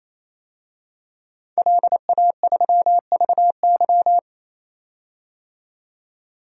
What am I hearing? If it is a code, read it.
Morse code, heard as a keyed tone, about 28 wpm, 700 Hz. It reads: LA3VY
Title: DXCC-100 Call Signs - 5 Characters - 28wpm